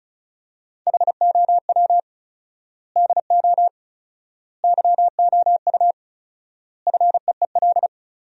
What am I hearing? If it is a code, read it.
Morse code, heard as a keyed tone, about 36 wpm, 700 Hz. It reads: HOW DO YOU FEEL
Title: 4x Word-Spacing - Sentences from Top 400 Words - 35wpm